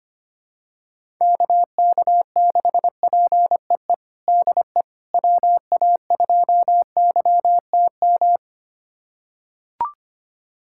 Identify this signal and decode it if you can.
Morse code, heard as a keyed tone, about 25 wpm, 700 Hz. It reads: KK6PEE DE WA2YTM
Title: USA Callsign DE Callsign - 6 Characters - 25wpm